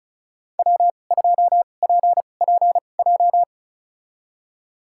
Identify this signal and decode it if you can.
Morse code, heard as a keyed tone, about 36 wpm, spaced as for 25 wpm, 700 Hz. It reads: W2PPJ
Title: USA Call Signs - 5 Characters - 2x Character Spacing - 35wpm